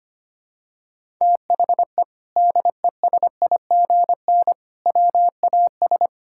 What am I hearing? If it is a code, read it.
Morse code, heard as a keyed tone, about 25 wpm, 700 Hz. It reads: THE DESIGN WAS
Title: Sentences from Top 1000 Words - Review Part 5 - 25wpm